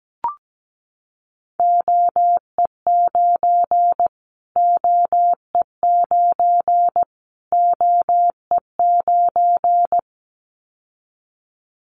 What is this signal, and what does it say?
Morse code, heard as a keyed tone, about 17 wpm, 700 Hz. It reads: OE9 OE9 OE9
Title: Callsign Prefixes for Top 25 DXCC - Repeated 3x - 17wpm